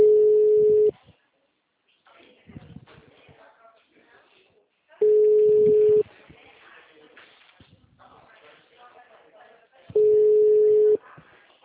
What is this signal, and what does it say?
Ringback tone (Europe, 425 Hz), with speech.